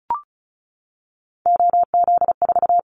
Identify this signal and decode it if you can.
Morse code, heard as a keyed tone, about 34 wpm, 700 Hz. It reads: OZ4
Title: Callsign Prefixes for Top 100 DXCC - 35wpm